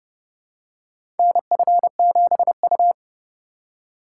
Morse code, heard as a keyed tone, about 30 wpm, 700 Hz. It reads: NF7U
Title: USA Call Signs - 3x Character Spacing - 30wpm